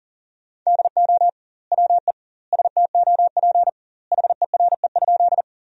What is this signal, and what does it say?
Morse code, heard as a keyed tone, about 40 wpm, 700 Hz. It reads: DO WE STOP HERE?
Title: 2x Word-Spacing - Sentences from Top 300 Words - 40wpm